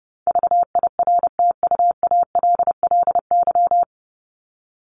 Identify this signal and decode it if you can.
Morse code, heard as a keyed tone, about 30 wpm, 700 Hz. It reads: VIRTUALLY
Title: Top 1501-1750 Words 30wpm